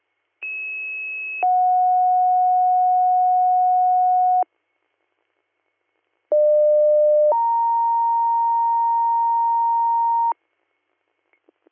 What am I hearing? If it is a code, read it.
Two-tone page, heard as a one-second tone, then a three-second tone, about 2600 then 730 Hz, about 600 then 930 Hz. It is heard from 0.4 to 10.3 s.